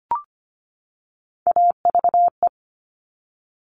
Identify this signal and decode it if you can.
Morse code, heard as a keyed tone, about 25 wpm, 700 Hz. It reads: AVE
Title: Top Three Letter Combinations in Words 25wpm